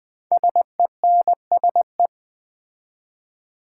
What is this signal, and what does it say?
Morse code, heard as a keyed tone, about 21 wpm, 700 Hz. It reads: SENSE
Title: 4x Word-Spacing - Sets of 2 Words - 20wpm